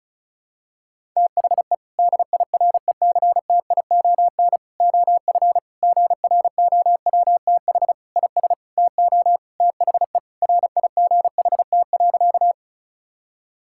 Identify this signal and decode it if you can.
Morse code, heard as a keyed tone, about 34 wpm, 700 Hz. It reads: THE DIRECTION OF GROWTH IS TO THE RIGHT.
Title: Sentences from Top 1000 Words - Review Part 5 - 35wpm